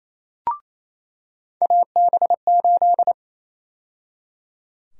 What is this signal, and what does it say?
Morse code, heard as a keyed tone, about 28 wpm, 700 Hz. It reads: AB8
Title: USA Call Sign Prefixes - 28wpm